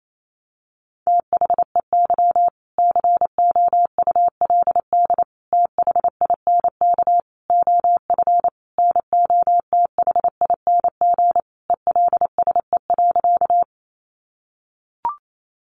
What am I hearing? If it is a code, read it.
Morse code, heard as a keyed tone, about 28 wpm, 700 Hz. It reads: THEY COULD THINK OF NOTHING ELSE.
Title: Sentences from Top 800 Words 28wpm